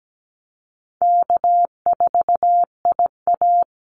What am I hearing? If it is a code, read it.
Morse code, heard as a keyed tone, about 17 wpm, 700 Hz. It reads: K4IA
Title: Easy Call Signs Repeated 3x - 17wpm